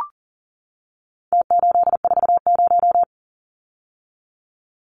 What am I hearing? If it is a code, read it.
Morse code, heard as a keyed tone, about 40 wpm, 700 Hz. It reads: T840
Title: Numbers - Contest Style - 4-Digit 40wpm